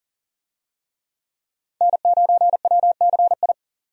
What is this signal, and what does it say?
Morse code, heard as a keyed tone, about 40 wpm, 700 Hz. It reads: N9WCI
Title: USA Call Signs - 5 Characters - 40wpm